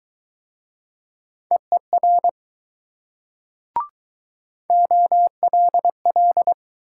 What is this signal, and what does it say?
Morse code, heard as a keyed tone, about 20 wpm, 700 Hz. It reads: EER OLL